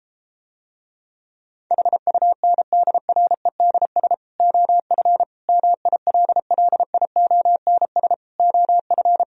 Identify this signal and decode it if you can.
Morse code, heard as a keyed tone, about 33 wpm, 700 Hz. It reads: HUNDREDS OF MILLIONS OF